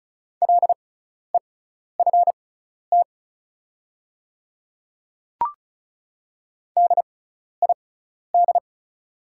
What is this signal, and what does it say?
Morse code, heard as a keyed tone, about 36 wpm, spaced as for 12 wpm, 690 Hz. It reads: LEFT DID
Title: 6x Character Spacing - Top 300 Words - 35wpm